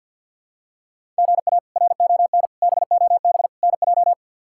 Morse code, heard as a keyed tone, about 51 wpm, 700 Hz. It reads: GA RON BOB NJ
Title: K1USN SST QSO Practice - 50wpm